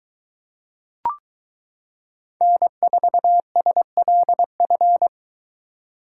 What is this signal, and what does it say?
Morse code, heard as a keyed tone, about 23 wpm, 700 Hz. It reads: N4SLF